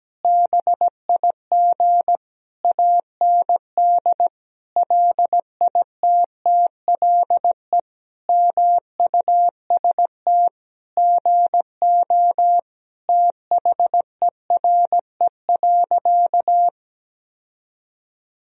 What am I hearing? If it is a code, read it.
Morse code, heard as a keyed tone, about 17 wpm, 700 Hz. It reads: BIG AND LITTLE MUST GO THERE.
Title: Sentences from Top 1000 Words - Review Part 1 - 17wpm